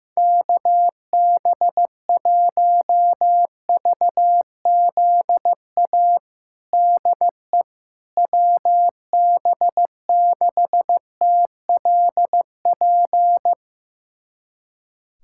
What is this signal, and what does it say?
Morse code, heard as a keyed tone, about 15 wpm, 700 Hz. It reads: KB1VZA DE WB6TLP
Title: USA Callsign DE Callsign - 6 Characters - 15wpm